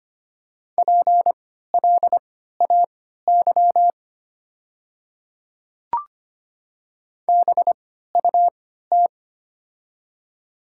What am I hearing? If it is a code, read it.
Morse code, heard as a keyed tone, about 25 wpm, spaced as for 14 wpm, 690 Hz. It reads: PLAY BUT
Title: 3x Character Spacing - Top 500 Words - 25wpm